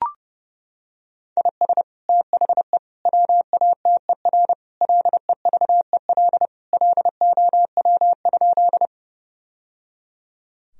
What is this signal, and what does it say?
Morse code, heard as a keyed tone, about 30 wpm, 700 Hz. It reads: IS THE WATER LEVEL LOW?
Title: Sentences from Top 1000 Words - Review Part 4 - 30wpm